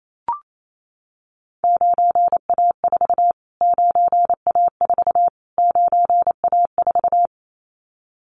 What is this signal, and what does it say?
Morse code, heard as a keyed tone, about 28 wpm, 700 Hz. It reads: 9A4 9A4 9A4
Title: Callsign Prefixes for Top 25 DXCC - Repeated 3x - 28wpm